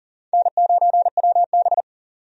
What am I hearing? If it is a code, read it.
Morse code, heard as a keyed tone, about 40 wpm, 700 Hz. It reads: N9WB
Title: USA Call Signs - 4 Characters - 40wpm